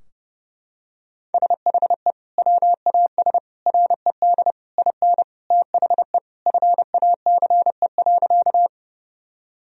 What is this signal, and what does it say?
Morse code, heard as a keyed tone, about 30 wpm, 700 Hz. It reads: SHE WAS RED IN THE FACE.
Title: Sentences from Top 1000 Words - Review Part 2 - 30wpm